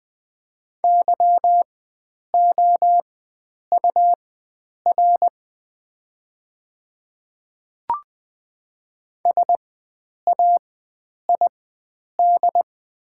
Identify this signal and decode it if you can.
Morse code, heard as a keyed tone, about 20 wpm, spaced as for 9 wpm, 700 Hz. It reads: YOUR SAID
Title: 4x Character Spacing - Top 100 Words - 20wpm